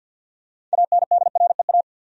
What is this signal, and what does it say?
Morse code, heard as a keyed tone, about 51 wpm, 700 Hz. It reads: ANDREA
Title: USA Common Names - 50wpm